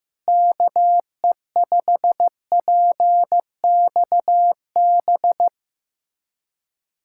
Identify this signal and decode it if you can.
Morse code, heard as a keyed tone, about 15 wpm, 700 Hz. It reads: KE5PXB
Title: USA Callsign DE Callsign 15wpm